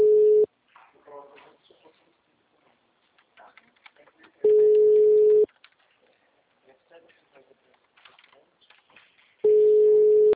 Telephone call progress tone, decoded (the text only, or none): ringback tone